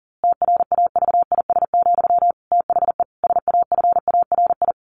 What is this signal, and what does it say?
Morse code, heard as a keyed tone, about 40 wpm, 700 Hz. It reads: TRAVIS, THE SAFARI
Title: A Sound Of Thunder 40wpm